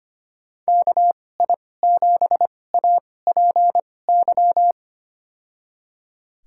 Morse code, heard as a keyed tone, about 25 wpm, spaced as for 18 wpm, 700 Hz. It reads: KI7APY
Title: USA Call Signs - 2x Character Spacing - 25wpm